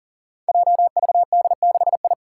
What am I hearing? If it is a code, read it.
Morse code, heard as a keyed tone, about 40 wpm, 700 Hz. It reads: JVD6I
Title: Five Challenging Letter-Number Combinations 40wpm